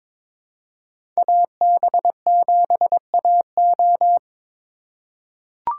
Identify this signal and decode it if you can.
Morse code, heard as a keyed tone, about 22 wpm, 700 Hz. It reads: AB7AO
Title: USA Call Signs - 5 Characters - 2x Character Spacing - 22wpm